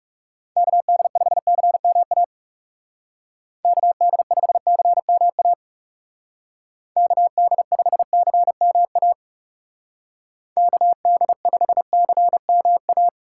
Morse code, heard as a keyed tone, about 40 wpm, 700 Hz. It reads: KD5CMA KD5CMA KD5CMA KD5CMA